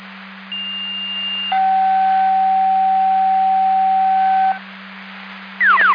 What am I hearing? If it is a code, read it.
Two-tone page, heard as a one-second tone, then a three-second tone, about 2800 then 760 Hz. It is heard from 0.5 to 4.5 s.